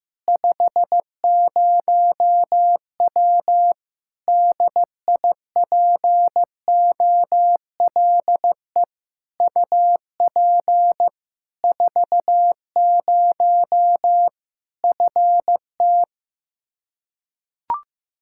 Morse code, heard as a keyed tone, about 16 wpm, 700 Hz. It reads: 50W DIPOLE UP 40 FT
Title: Practice QSO - Exchange 2 - Info - 15wpm